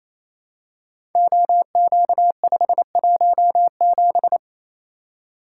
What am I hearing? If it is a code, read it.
Morse code, heard as a keyed tone, about 28 wpm, 700 Hz. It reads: OQ517
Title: Code Groups - 5 Letter:Numbers - 28wpm